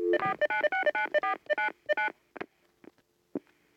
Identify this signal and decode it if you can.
DTMF, heard as telephone touch tones, tones of about 120 ms. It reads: 4969499